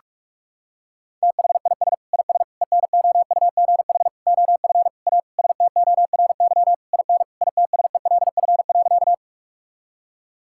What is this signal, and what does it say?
Morse code, heard as a keyed tone, about 42 wpm, 700 Hz. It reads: THIS IS ENOUGH OF A STORY IN ITSELF.
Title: Sentences from Top 1000 Words - Review Part 5 - 45wpm